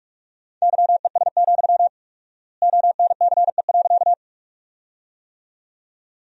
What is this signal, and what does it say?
Morse code, heard as a keyed tone, about 46 wpm, 700 Hz. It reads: YES, ONCE.
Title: 4x Word-Spacing - Sentences from Top 500 Words - 45wpm